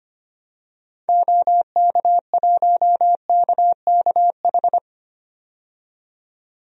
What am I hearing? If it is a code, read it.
Morse code, heard as a keyed tone, about 25 wpm, 690 Hz. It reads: OK1KKH
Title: DXCC-100 Call Signs - 6 Characters - 25wpm